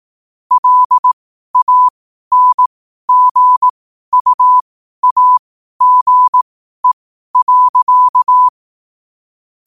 Morse code, heard as a keyed tone, about 18 wpm, spaced as for 12 wpm, 1000 Hz. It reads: LANGUAGE.
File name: qotdfiles / 12wpm